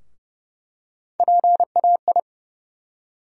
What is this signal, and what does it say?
Morse code, heard as a keyed tone, about 31 wpm, 700 Hz. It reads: PAI
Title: USA Call Sign Postfixes - 3 Letters 30wpm